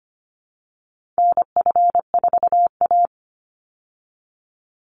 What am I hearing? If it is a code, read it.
Morse code, heard as a keyed tone, about 25 wpm, 700 Hz. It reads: NF4A
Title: Call Signs Repeated 3x - 25wpm